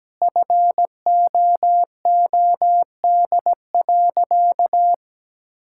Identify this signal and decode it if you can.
Morse code, heard as a keyed tone, about 17 wpm, 700 Hz. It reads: FOOD.
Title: Sentences from Top 1000 Words - Review Part 5 - 17wpm